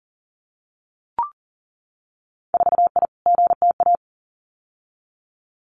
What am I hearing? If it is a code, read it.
Morse code, heard as a keyed tone, about 37 wpm, 700 Hz. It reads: 4I GTA
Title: Winter Field Day Exchanges Encore 40wpm